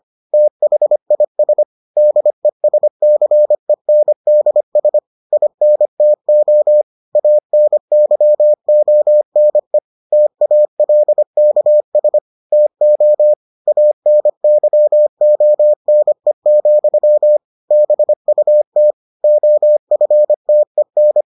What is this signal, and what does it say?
Morse code, heard as a keyed tone, about 25 wpm, 600 Hz. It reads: THIS DESCENDS INTO ANYONE TALKS TO ANYONE, BUT OFTEN